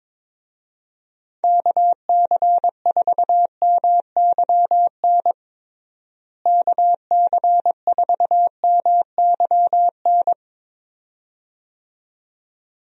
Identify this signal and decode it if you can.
Morse code, heard as a keyed tone, about 21 wpm, 700 Hz. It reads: KC4MYN KC4MYN
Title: USA Call Signs - 6 Characters - Repeated 2x - 22wpm